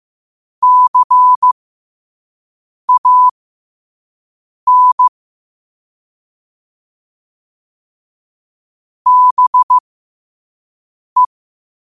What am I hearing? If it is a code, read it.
Morse code, heard as a keyed tone, about 15 wpm, spaced as for 5 wpm, 1000 Hz. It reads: CAN BE
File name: qotdfiles / 5wpm